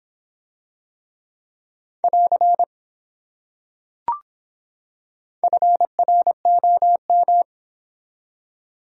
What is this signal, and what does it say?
Morse code, heard as a keyed tone, about 24 wpm, 700 Hz. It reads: <AR> FROM